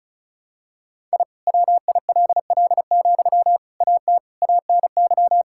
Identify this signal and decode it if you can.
Morse code, heard as a keyed tone, about 36 wpm, 700 Hz. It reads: I WILL, AT ANY
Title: Sentences from Top 1000 Words - Review Part 5 - 35wpm